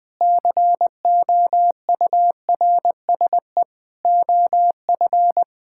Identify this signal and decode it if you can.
Morse code, heard as a keyed tone, about 20 wpm, 700 Hz. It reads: COURSE OF
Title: Build A Phrase - 5-Word Phrases - 20wpm